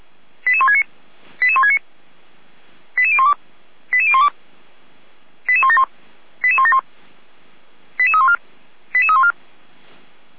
Five-tone selective call (ZVEI), heard as quick runs of five tones, tones of about 70 ms. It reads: 88169 88169 88012 88012 88171 88171 88314 88314